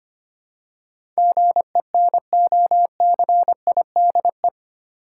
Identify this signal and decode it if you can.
Morse code, heard as a keyed tone, about 25 wpm, 700 Hz. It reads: GENOCIDE
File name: 8 Letter Words - First 3 Letters Given - 25wpm